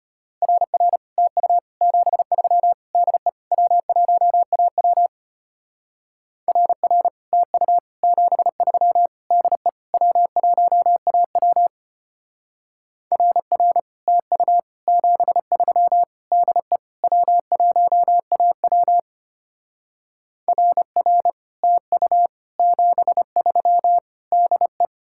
Morse code, heard as a keyed tone, about 32 wpm, 700 Hz. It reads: RR TU 73 DE W1AW RR TU 73 DE W1AW RR TU 73 DE W1AW RR TU 73 DE